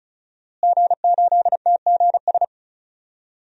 Morse code, heard as a keyed tone, about 36 wpm, 700 Hz. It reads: G8TGS
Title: DXCC-100 Call Signs - 5 Characters - Repeated 2x - 35wpm